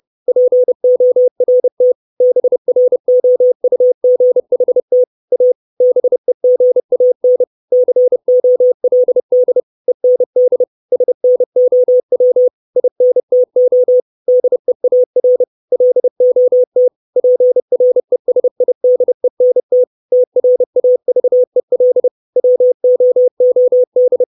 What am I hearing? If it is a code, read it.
Morse code, heard as a keyed tone, about 30 wpm, 500 Hz. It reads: PORT BROUGHT A BEGAN COLD END SNOW INTO DEAR LOT PRESIDENT TRAVEL WOOD
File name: 632 Words 30WPM